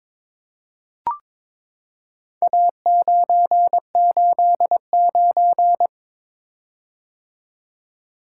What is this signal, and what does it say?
Morse code, heard as a keyed tone, about 23 wpm, 700 Hz. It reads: A989
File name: practice / 22wpm / Four Letter-Number Combinations - 22wpm